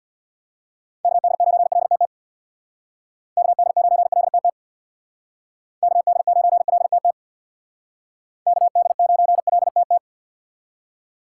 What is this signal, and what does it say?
Morse code, heard as a keyed tone, about 59 wpm, 700 Hz. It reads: KD9LTT KD9LTT KD9LTT KD9LTT
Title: Speed Racing - USA 2x3 Call Signs 50wpm